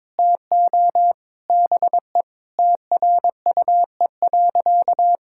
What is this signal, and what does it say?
Morse code, heard as a keyed tone, about 22 wpm, 700 Hz. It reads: TO BE TRUE.
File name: practice / 22wpm / Sentences with 4-Word Phrases - 22wpm